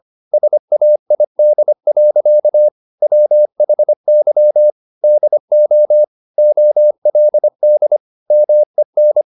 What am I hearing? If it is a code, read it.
Morse code, heard as a keyed tone, about 25 wpm, 600 Hz. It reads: SAID. WHY DO OLD MEN
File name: the old man and the sea part 1 25wpm